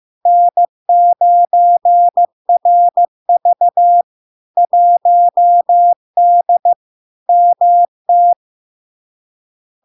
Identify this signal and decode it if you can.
Morse code, heard as a keyed tone, about 15 wpm, 700 Hz. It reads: N9RV 1D MT
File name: LARK-FD-2022-Log-in-CW-15wpm-700-Hz